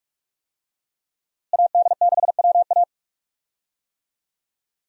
Morse code, heard as a keyed tone, about 46 wpm, 700 Hz. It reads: AD6WA
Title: USA Call Signs - 5 Characters - Repeated 3x - 45wpm